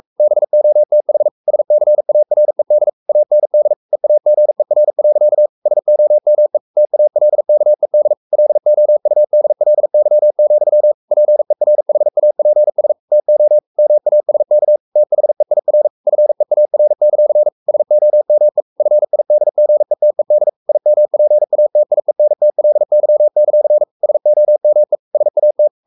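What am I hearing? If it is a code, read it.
Morse code, heard as a keyed tone, about 44 wpm, 600 Hz. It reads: BOTH SCARED AND EAGER. SOME TALKED LOUDLY, PERHAPS TO MASK THEIR FEAR; SOME FIDGETED IMPATIENTLY; SOME SAT
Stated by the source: Morse code at 43 wpm sending Ken Follett's novel Never in English, from CW med Gnister.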